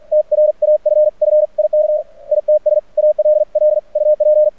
Morse code, heard as a keyed tone, about 46 wpm, 610 Hz. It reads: TRAFFIK ITS ALL UP
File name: archived / CRSnet